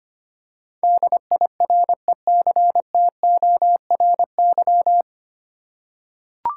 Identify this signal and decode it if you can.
Morse code, heard as a keyed tone, about 25 wpm, 700 Hz. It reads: DIRECTORY
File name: Nine Letter Words - 25wpm